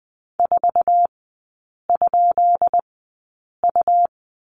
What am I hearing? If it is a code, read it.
Morse code, heard as a keyed tone, about 20 wpm, spaced as for 8 wpm, 700 Hz. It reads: 4?U
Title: 166 Lesson 27e - Number 7 - Three Characters 20wpm